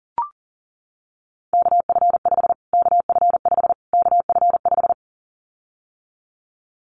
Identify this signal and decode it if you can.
Morse code, heard as a keyed tone, about 40 wpm, 700 Hz. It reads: KF5 KF5 KF5
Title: Callsign Prefixes for Top 25 DXCC - Repeated 3x - 40wpm